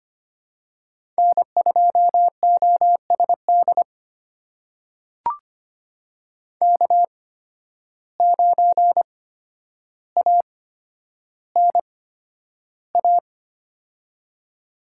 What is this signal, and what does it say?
Morse code, heard as a keyed tone, about 25 wpm, spaced as for 11 wpm, 700 Hz. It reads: N2OSD K9ANA
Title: USA Call Signs - 5 Characters - 8x Character Spacing - 25wpm